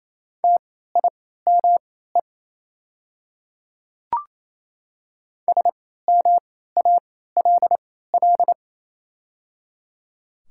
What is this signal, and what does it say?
Morse code, heard as a keyed tone, about 28 wpm, spaced as for 16 wpm, 700 Hz. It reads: TIME SMALL
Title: 3x Character Spacing - Top 200 Words - 28wpm